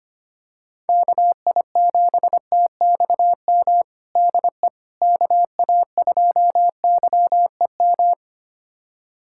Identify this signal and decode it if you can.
Morse code, heard as a keyed tone, about 25 wpm, 700 Hz. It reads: KI7TXM DE KA2YEM
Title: USA Callsign DE Callsign - 6 Characters - 25wpm